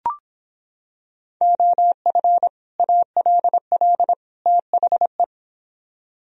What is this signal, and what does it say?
Morse code, heard as a keyed tone, about 26 wpm, 700 Hz. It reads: OF ALL THE